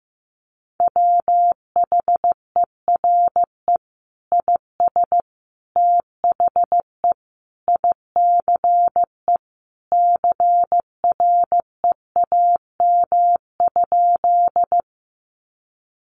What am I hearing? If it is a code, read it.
Morse code, heard as a keyed tone, about 15 wpm, 700 Hz. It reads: WHERE IS THE ICE CREAM?
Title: Sentences from Top 1500 Words 15wpm